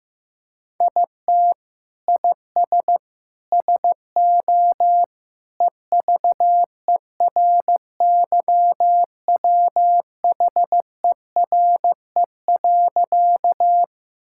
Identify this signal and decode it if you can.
Morse code, heard as a keyed tone, about 15 wpm, 700 Hz. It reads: IT IS SO EVERYWHERE.
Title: Sentences from Top 1600 Words 15wpm